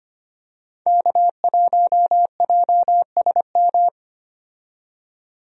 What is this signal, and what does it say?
Morse code, heard as a keyed tone, about 25 wpm, 700 Hz. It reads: K1JSM
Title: USA Call Signs - 5 Characters - 25wpm